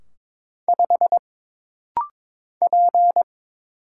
Morse code, heard as a keyed tone, about 22 wpm, 700 Hz. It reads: 5 P
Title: Single Letter-Number - Rapid-Fire 22wpm